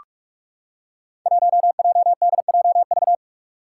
Morse code, heard as a keyed tone, about 44 wpm, 700 Hz. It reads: 1JDJV
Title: Five Challenging Letter-Number Combinations 45wpm